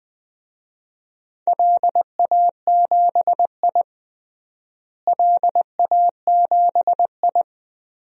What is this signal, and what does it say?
Morse code, heard as a keyed tone, about 20 wpm, 700 Hz. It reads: LA7I LA7I
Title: DXCC-100 Call Signs - 4 Characters - Repeated 3x - 20wpm